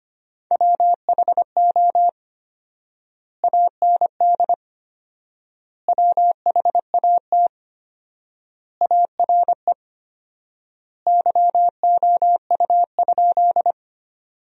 Morse code, heard as a keyed tone, about 25 wpm, 700 Hz. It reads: WHO AND WHAT ARE YOU?
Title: 4x Word-Spacing - Sentences from Top 100 Words - 25wpm